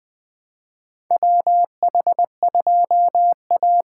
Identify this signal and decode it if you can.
Morse code, heard as a keyed tone, about 20 wpm, 700 Hz. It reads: WH2A
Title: USA Call Signs - 6x Character Spacing - 20wpm